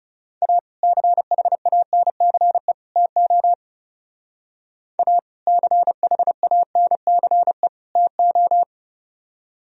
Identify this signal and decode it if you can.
Morse code, heard as a keyed tone, about 35 wpm, 700 Hz. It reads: A CHANCE TO A CHANCE TO
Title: Speed-Racing - Sets of 3 Words 30wpm